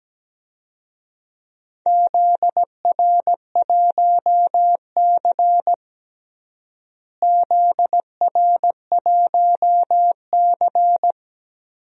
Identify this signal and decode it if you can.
Morse code, heard as a keyed tone, about 17 wpm, 700 Hz. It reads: ZR1C ZR1C
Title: DXCC-100 Call Signs - 4 Characters - Repeated 3x - 17wpm